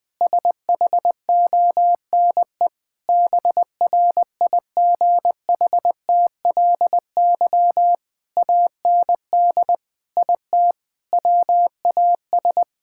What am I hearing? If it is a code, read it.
Morse code, heard as a keyed tone, about 20 wpm, 700 Hz. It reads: SHONE BRIGHTLY AND IT WAS
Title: Sentences with 2-Word Phrases - 20wpm